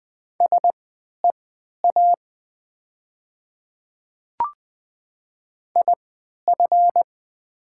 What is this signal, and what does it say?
Morse code, heard as a keyed tone, about 20 wpm, spaced as for 11 wpm, 700 Hz. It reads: SEA IF
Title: 3x Character Spacing - Top 300 Words - 20wpm